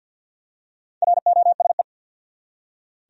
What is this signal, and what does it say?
Morse code, heard as a keyed tone, about 46 wpm, 700 Hz. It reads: ROSE
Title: USA Common Names - Max 4 Letters - 50wpm